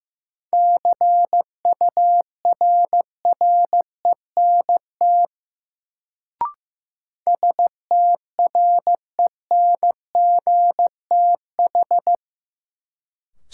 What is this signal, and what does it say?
Morse code, heard as a keyed tone, about 15 wpm, 700 Hz. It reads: CURRENT STRENGTH
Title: Sentences from Top 1000 Words - Review Part 4 - New Word Drill - 15wpm